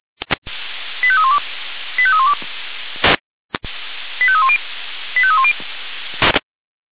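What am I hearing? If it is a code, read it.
Five-tone selective call (ZVEI), heard as quick runs of five tones, tones of about 70 ms. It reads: 85312 85312 85310 85310
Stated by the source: ZVEI five-tone call 85320 received on SDR.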